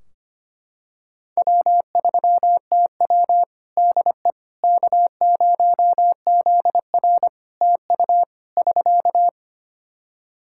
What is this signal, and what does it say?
Morse code, heard as a keyed tone, about 25 wpm, 700 Hz. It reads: W3TW DE K0ZR TU <SK>
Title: K1USN SST QSO Practice 25wpm